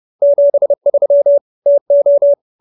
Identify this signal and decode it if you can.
Morse code, heard as a keyed tone, about 30 wpm, 570 Hz. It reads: 73 TO
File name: QSO315 30